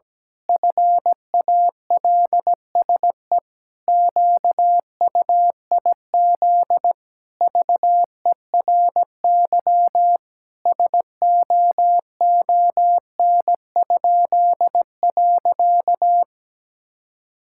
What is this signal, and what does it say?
Morse code, heard as a keyed tone, about 17 wpm, 700 Hz. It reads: FALSE QUIZ VERY SOON?.
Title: Pangram Challenge v2 - 17wpm